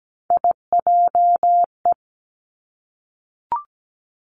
Morse code, heard as a keyed tone, about 17 wpm, 700 Hz. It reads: IJE
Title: Three Challenging Letter-Number Combinations 17wpm